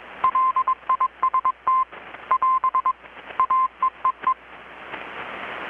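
Morse code, heard as a keyed tone, about 21 wpm, 1030 Hz. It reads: LIST <AS> AEEE
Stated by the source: Morse from KPH's Night of Nights XX maritime radiotelegraphy event, received in Chiapas, Mexico about 3,800 km away on a Kenwood R-600 with a grounded 10 m long-wire antenna.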